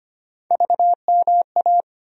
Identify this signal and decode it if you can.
Morse code, heard as a keyed tone, about 25 wpm, 700 Hz. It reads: VMA
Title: USA Call Sign Postfixes - 3 Letters 25wpm